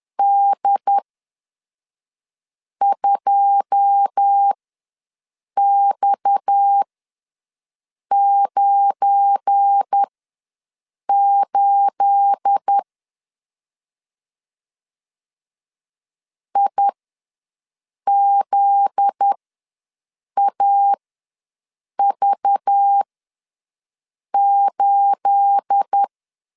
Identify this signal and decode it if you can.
Morse code, heard as a keyed tone, about 10 wpm, spaced as for 5 wpm, 800 Hz. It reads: D2X98 IZAV8
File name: LEZIONI / 9 28 ws 10wpm 4124